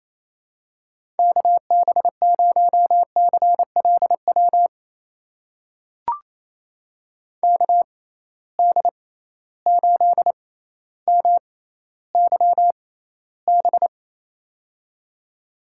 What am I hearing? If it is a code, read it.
Morse code, heard as a keyed tone, about 28 wpm, spaced as for 14 wpm, 700 Hz. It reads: KB0CLW KD8MYB
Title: USA Call Signs - 6 Characters - 6x Character Spacing - 28wpm